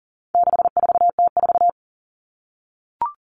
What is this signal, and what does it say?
Morse code, heard as a keyed tone, about 40 wpm, 700 Hz. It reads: 64T4
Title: All Cut Numbers - 4-Digit 40wpm